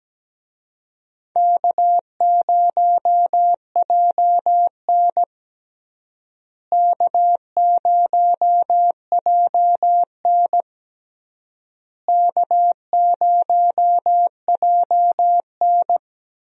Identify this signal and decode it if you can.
Morse code, heard as a keyed tone, about 17 wpm, 700 Hz. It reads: K0JN K0JN K0JN
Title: USA Call Signs - 4 Characters - Repeated 3x - 17wpm